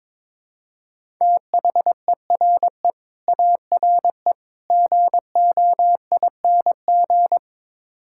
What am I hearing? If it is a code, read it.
Morse code, heard as a keyed tone, about 22 wpm, 700 Hz. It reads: THERE ARE GOING
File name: practice / 22wpm / Build A Phrase - 5-Word Phrases - 22wpm